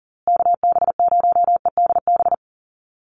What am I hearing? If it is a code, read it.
Morse code, heard as a keyed tone, about 40 wpm, 700 Hz. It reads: KB0EDB
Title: USA Call Signs Repeated 3x 40wpm